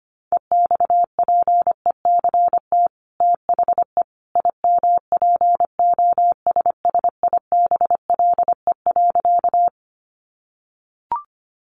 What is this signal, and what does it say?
Morse code, heard as a keyed tone, about 25 wpm, 700 Hz. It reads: EXPECT THE IMPOSSIBLE.
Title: Sentences from Top 1100 Words 25wpm